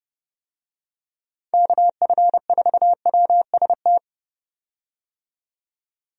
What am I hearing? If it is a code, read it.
Morse code, heard as a keyed tone, about 30 wpm, 700 Hz. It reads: KF4WST
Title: USA Call Signs - 6 Characters - 30wpm